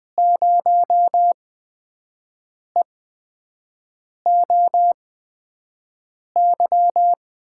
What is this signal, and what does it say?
Morse code, heard as a keyed tone, about 20 wpm, spaced as for 5 wpm, 700 Hz. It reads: 0EOY
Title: USA Call Signs - 8x Character Spacing - 20wpm